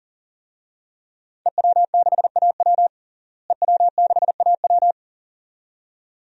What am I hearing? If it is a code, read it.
Morse code, heard as a keyed tone, about 40 wpm, 700 Hz. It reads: EW6AW EW6AW
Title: DXCC-100 Call Signs - 5 Characters - Repeated 2x - 40wpm